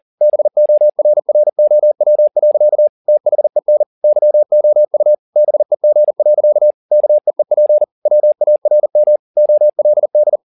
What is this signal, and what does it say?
Morse code, heard as a keyed tone, about 40 wpm, 600 Hz. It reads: BORROW. THEN YOU BEG. KEEP WARM OLD